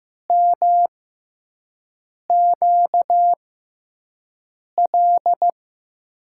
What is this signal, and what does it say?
Morse code, heard as a keyed tone, about 15 wpm, spaced as for 5 wpm, 700 Hz. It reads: MQL
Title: USA Call Signs - 6x Character Spacing - 15wpm